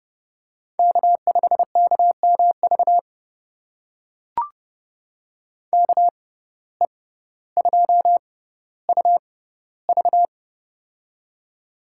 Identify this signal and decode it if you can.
Morse code, heard as a keyed tone, about 30 wpm, spaced as for 15 wpm, 700 Hz. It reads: K5KMV KE2UV
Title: USA Call Signs - 5 Characters - 6x Character Spacing - 30wpm